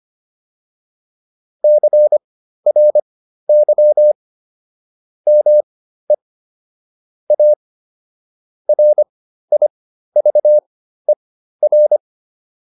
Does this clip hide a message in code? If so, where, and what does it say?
Yes: Morse code from 1.6 to 12.0 s, heard as a keyed tone, about 25 wpm, spaced as for 13 wpm, 600 Hz. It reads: CRY ME A RIVER